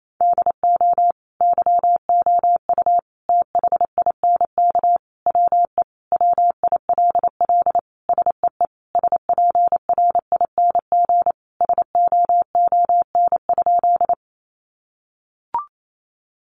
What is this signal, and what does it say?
Morse code, heard as a keyed tone, about 28 wpm, 700 Hz. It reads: DO YOU THINK WE WILL SEE SPRING SOON?
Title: Sentences from Top 1200 Words 28wpm